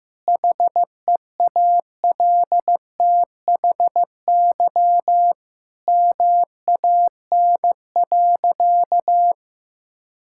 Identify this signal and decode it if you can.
Morse code, heard as a keyed tone, about 15 wpm, 700 Hz. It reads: HEALTHY MAN.
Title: Sentences from Top 1000 Words - Review Part 5 - 15wpm